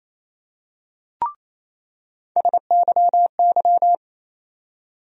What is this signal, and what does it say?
Morse code, heard as a keyed tone, about 28 wpm, 700 Hz. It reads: SYY
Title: USA Call Sign Postfixes - 3 Letters 28wpm